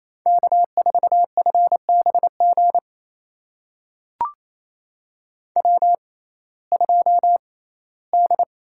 Morse code, heard as a keyed tone, about 28 wpm, spaced as for 17 wpm, 700 Hz. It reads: K4FBG W2D
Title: USA Call Signs - 5 Characters - 6x Character Spacing - 28wpm